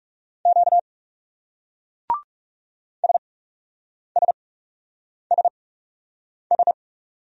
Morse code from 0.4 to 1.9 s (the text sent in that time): Q